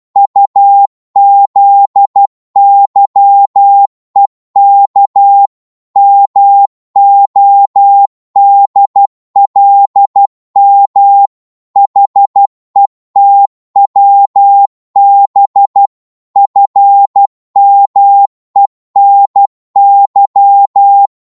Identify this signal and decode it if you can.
Morse code, heard as a keyed tone, about 12 wpm, 800 Hz. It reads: UZYEK MODLM HETWB FMENY
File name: Morse / ltr-12wpm